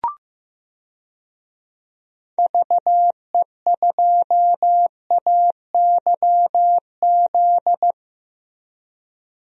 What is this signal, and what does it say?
Morse code, heard as a keyed tone, about 15 wpm, 700 Hz. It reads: VE2AYZ
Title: DXCC-100 Call Signs - 6 Characters - Repeated 3x - 15wpm